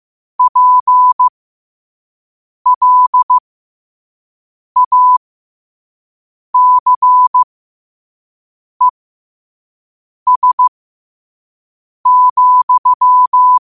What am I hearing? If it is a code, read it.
Morse code, heard as a keyed tone, about 15 wpm, spaced as for 5 wpm, 1000 Hz. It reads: PLACES,
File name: qotdfiles / 5wpm